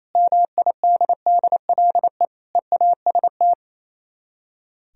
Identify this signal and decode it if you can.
Morse code, heard as a keyed tone, about 35 wpm, 700 Hz. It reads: MIDDLE EAST